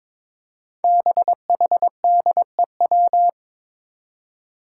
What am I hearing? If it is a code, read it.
Morse code, heard as a keyed tone, about 21 wpm, 700 Hz. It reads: BHDEW
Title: Five Challenging Letter-Number Combinations 22wpm